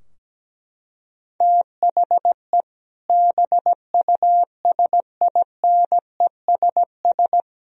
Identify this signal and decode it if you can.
Morse code, heard as a keyed tone, about 17 wpm, 700 Hz. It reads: THE BUSINESS
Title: Sentences from Top 1000 Words - Review Part 4 - 17wpm